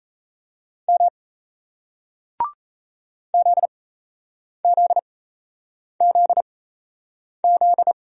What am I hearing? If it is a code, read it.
Morse code, heard as a keyed tone, about 37 wpm, spaced as for 8 wpm, 700 Hz. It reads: M ZZZZ